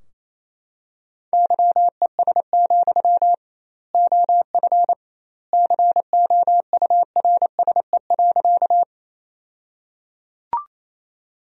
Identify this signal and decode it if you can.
Morse code, heard as a keyed tone, about 28 wpm, 700 Hz. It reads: YES, OF COURSE.
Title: 2x Word-Spacing - Sentences from Top 500 Words - 28wpm